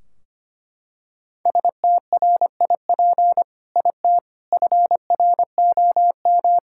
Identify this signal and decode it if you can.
Morse code, heard as a keyed tone, about 25 wpm, 700 Hz. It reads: STRIP IT FROM
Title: Sentences from Top 1600 Words 25wpm